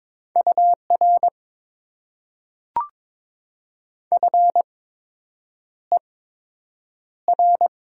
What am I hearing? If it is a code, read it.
Morse code, heard as a keyed tone, about 22 wpm, spaced as for 8 wpm, 700 Hz. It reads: UR FER
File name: Top 90 QSO Elements - 8x Character Spacing - 22wpm